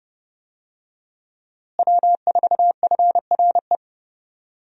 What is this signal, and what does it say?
Morse code, heard as a keyed tone, about 31 wpm, 700 Hz. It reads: W4FRE
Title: USA Call Signs - 5 Characters - Repeated 3x - 30wpm